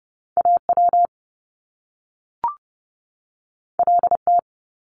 Morse code, heard as a keyed tone, about 30 wpm, 700 Hz. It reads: AW LT
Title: Top Two Letter Combinations in Words 30wpm